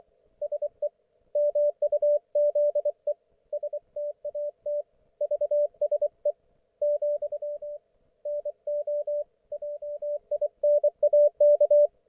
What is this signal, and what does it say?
Morse code, heard as a keyed tone, about 24 wpm, 590 Hz. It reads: SE MUZE STAT VSE , NO JINAK
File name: Archiv / CUC212 / ok1fhi cul212